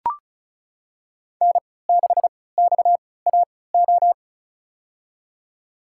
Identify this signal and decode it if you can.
Morse code, heard as a keyed tone, about 34 wpm, spaced as for 20 wpm, 700 Hz. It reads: N6XAO
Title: USA Call Signs - 5 Characters - 3x Character Spacing - 35wpm